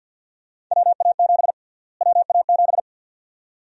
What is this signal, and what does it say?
Morse code, heard as a keyed tone, about 52 wpm, 700 Hz. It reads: WA7 WA7
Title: USA Call Sign Prefixes - Repeated 2x - 50wpm